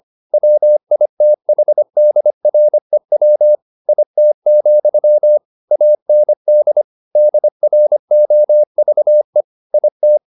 Morse code, heard as a keyed tone, about 25 wpm, 600 Hz. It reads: WITHDREW IT, AND DROVE IT